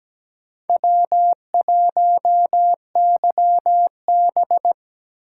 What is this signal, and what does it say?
Morse code, heard as a keyed tone, about 17 wpm, 700 Hz. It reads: W1YB
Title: USA Call Signs - 4 Characters - 2x Character Spacing - 17wpm